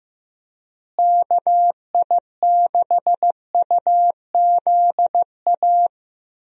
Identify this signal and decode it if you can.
Morse code, heard as a keyed tone, about 15 wpm, 700 Hz. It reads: KI6UZA
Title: USA Call Signs - 6 Characters - Repeated 2x - 15wpm